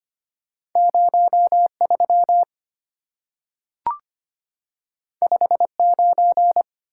Morse code, heard as a keyed tone, about 25 wpm, 700 Hz. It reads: 03 59
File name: Two Letter-Number Combinations - 25wpm